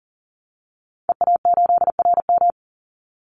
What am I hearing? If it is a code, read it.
Morse code, heard as a keyed tone, about 40 wpm, 700 Hz. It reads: EA8RM
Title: Call Signs - 40wpm